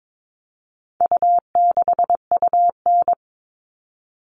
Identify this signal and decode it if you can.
Morse code, heard as a keyed tone, about 22 wpm, 700 Hz. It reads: U6UN
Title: All Cut Numbers - 4-Digit 22wpm